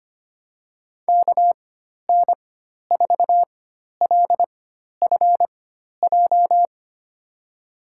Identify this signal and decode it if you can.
Morse code, heard as a keyed tone, about 25 wpm, spaced as for 12 wpm, 700 Hz. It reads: KN4LFJ
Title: USA Call Signs - 6 Characters - 4x Character Spacing - 25wpm